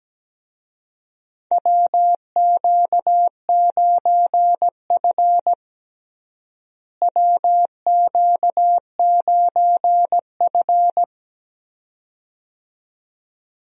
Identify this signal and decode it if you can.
Morse code, heard as a keyed tone, about 17 wpm, 700 Hz. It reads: WQ9F WQ9F
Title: USA Call Signs - 4 Characters - Repeated 2x - 17wpm